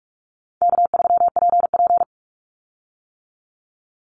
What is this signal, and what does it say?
Morse code, heard as a keyed tone, about 44 wpm, 700 Hz. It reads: K3PP
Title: Easy Call Signs - 45wpm